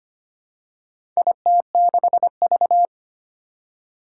Morse code, heard as a keyed tone, about 25 wpm, 700 Hz. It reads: IT6V
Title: Four Challenging Letter-Number Combinations 25wpm